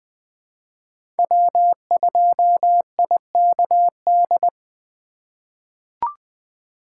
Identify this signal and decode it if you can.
Morse code, heard as a keyed tone, about 20 wpm, 700 Hz. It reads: W2IKD
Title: USA Call Signs - 4x Character Spacing - 20wpm